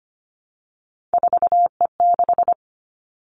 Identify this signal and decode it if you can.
Morse code, heard as a keyed tone, about 25 wpm, 700 Hz. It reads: 4E6
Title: Three Challenging Letter-Number Combinations 25wpm